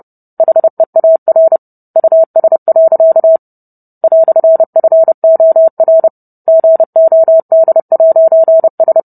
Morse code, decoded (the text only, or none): HEAR US. "FOR GOD'S